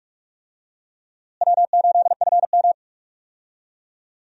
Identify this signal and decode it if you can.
Morse code, heard as a keyed tone, about 47 wpm, 700 Hz. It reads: W8FM
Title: USA Call Signs - 4 Characters - Repeated 2x - 45wpm